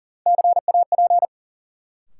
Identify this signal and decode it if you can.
Morse code, heard as a keyed tone, about 40 wpm, 700 Hz. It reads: CAP
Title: Sentences from Top 1600 Words - New Word Drill Alt - 40wpm